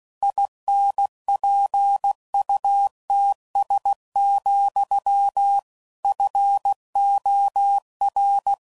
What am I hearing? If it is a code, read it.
Morse code, heard as a keyed tone, about 16 wpm, 800 Hz. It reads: INPUTS, FOR
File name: CW Code Groups @15WPM / Technical Subjects / AND Gate